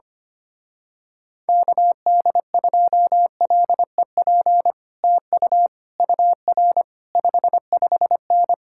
Morse code, heard as a keyed tone, about 24 wpm, 700 Hz. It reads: KD2LEP TU UR 55N
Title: Speed-Racing Common QSO Phrases Encore - 25wpm